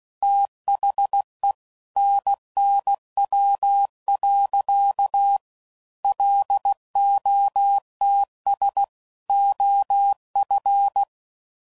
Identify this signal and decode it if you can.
Morse code, heard as a keyed tone, about 16 wpm, 800 Hz. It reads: THE NNW. LOTS OF
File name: CW Code Groups @15WPM / Weather Sets / Weather 5 12